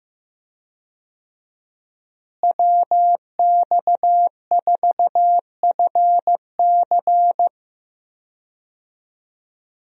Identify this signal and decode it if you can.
Morse code, heard as a keyed tone, about 15 wpm, 700 Hz. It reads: WX4FC
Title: USA Call Signs - 5 Characters - Repeated 3x - 15wpm